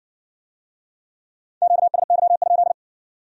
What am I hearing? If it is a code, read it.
Morse code, heard as a keyed tone, about 53 wpm, 700 Hz. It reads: QSY?